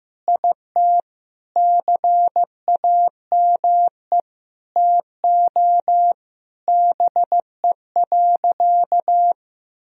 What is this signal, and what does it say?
Morse code, heard as a keyed tone, about 15 wpm, 700 Hz. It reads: IT CAME TO BE.
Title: Sentences from Top 1000 Words - Review Part 3 - 15wpm